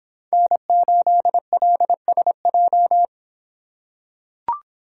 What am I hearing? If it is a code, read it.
Morse code, heard as a keyed tone, about 26 wpm, 700 Hz. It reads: N8LSJ